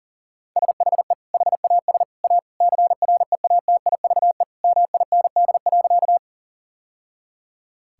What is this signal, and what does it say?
Morse code, heard as a keyed tone, about 37 wpm, 700 Hz. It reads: SHE HAS A CREATIVE MIND.
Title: Sentences from Top 1000 Words - Review Part 5 - 40wpm